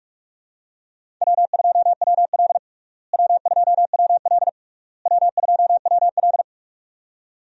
Morse code, heard as a keyed tone, about 41 wpm, 700 Hz. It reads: W2WL W2WL W2WL
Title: USA Call Signs - 4 Characters - Repeated 3x - 45wpm